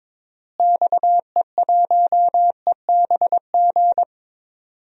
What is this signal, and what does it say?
Morse code, heard as a keyed tone, about 22 wpm, 700 Hz. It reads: XE1EBG
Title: DXCC-100 Call Signs - 6 Characters - Repeated 2x - 22wpm